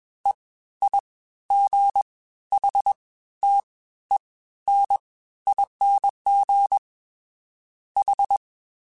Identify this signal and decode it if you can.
Morse code, heard as a keyed tone, about 21 wpm, spaced as for 13 wpm, 800 Hz. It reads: EIGHTENING H